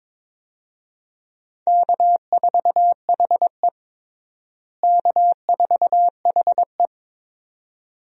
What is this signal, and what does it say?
Morse code, heard as a keyed tone, about 22 wpm, 700 Hz. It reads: K4HE K4HE
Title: USA Call Signs - 4 Characters - Repeated 2x - 22wpm